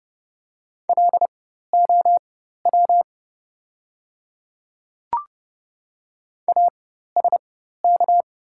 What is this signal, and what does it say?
Morse code, heard as a keyed tone, about 30 wpm, spaced as for 14 wpm, 700 Hz. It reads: LOW ASK
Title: 4x Character Spacing - Top 200 Words - 30wpm